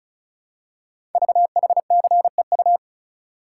Morse code, heard as a keyed tone, about 36 wpm, 700 Hz. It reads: VHCEU
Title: Code Groups - 5 Letters - 35wpm